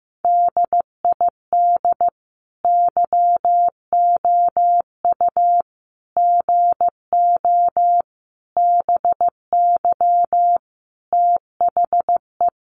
Morse code, heard as a keyed tone, about 15 wpm, 700 Hz. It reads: DID YOU GO BY THE
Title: Sentences from Top 1200 Words 15wpm